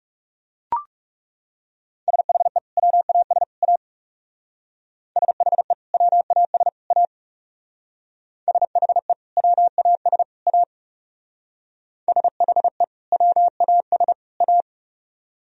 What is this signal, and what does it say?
Morse code, heard as a keyed tone, about 39 wpm, 700 Hz. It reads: SHE WAS A SHE WAS A SHE WAS A SHE WAS A